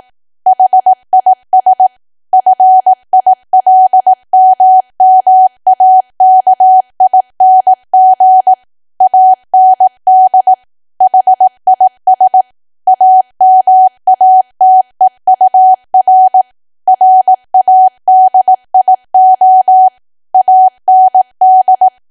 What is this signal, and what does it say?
Morse code, heard as a keyed tone, about 18 wpm, 750 Hz. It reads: HIS FILMMAKING AND HIS AMATEUR RADIO AND
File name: Archive / 18 WPM / 18WPM